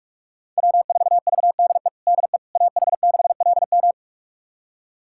Morse code, heard as a keyed tone, about 42 wpm, 700 Hz. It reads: W4VDE DE AH6LM
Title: USA Callsign DE Callsign - 5 Characters - 45wpm